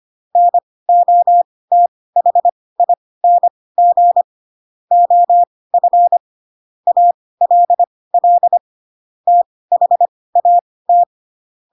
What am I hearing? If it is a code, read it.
Morse code, heard as a keyed tone, about 25 wpm, spaced as for 18 wpm, 700 Hz. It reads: NOTHING OF ALL THAT
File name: Pinocchio / Pinocchio ch7 12 25 ik1hge